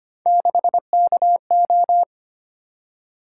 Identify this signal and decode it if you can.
Morse code, heard as a keyed tone, about 25 wpm, 700 Hz. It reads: 6KO
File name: Three Letter-Number Combinations - 25wpm